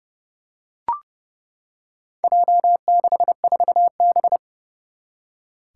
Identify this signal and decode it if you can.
Morse code, heard as a keyed tone, about 30 wpm, 700 Hz. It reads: J64B
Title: Four Challenging Letter-Number Combinations 30wpm